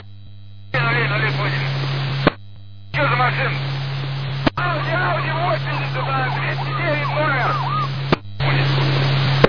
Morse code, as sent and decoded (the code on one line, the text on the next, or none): none